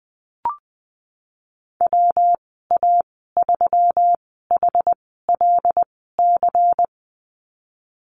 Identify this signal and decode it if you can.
Morse code, heard as a keyed tone, about 20 wpm, spaced as for 14 wpm, 700 Hz. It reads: WA3HLC
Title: 122 Lesson 20g - Number 3 - Call Signs 20wpm